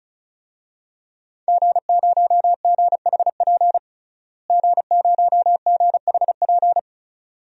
Morse code, heard as a keyed tone, about 34 wpm, 700 Hz. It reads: G0GHP G0GHP
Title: DXCC-100 Call Signs - 5 Characters - Repeated 2x - 35wpm